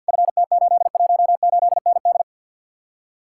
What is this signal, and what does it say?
Morse code, heard as a keyed tone, about 64 wpm, 700 Hz. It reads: FT818ND